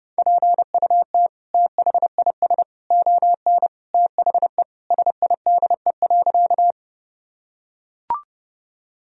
Morse code, heard as a keyed tone, about 31 wpm, 700 Hz. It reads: PUT THIS ON THE SIDE.
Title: Sentences from Top 1000 Words - Review Part 1 - 30wpm